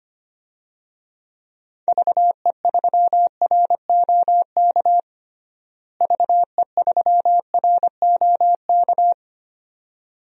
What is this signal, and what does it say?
Morse code, heard as a keyed tone, about 25 wpm, 700 Hz. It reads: VE3ROK VE3ROK
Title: DXCC-100 Call Signs - 6 Characters - Repeated 2x - 25wpm